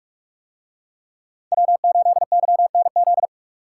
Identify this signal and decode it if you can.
Morse code, heard as a keyed tone, about 43 wpm, 700 Hz. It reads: W8YNZ